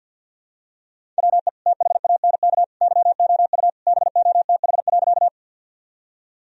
Morse code, heard as a keyed tone, about 51 wpm, 700 Hz. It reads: WE THANK YOU BOTH.
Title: Sentences from Top 1000 Words - Review Part 1 - 50wpm